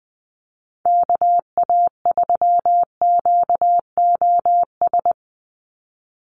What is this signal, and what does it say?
Morse code, heard as a keyed tone, about 20 wpm, 700 Hz. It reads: KA3QOS
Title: USA Call Signs 20wpm